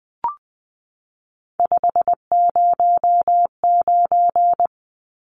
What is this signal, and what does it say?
Morse code, heard as a keyed tone, about 20 wpm, 700 Hz. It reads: 509
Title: Numbers - 3-Digit 20wpm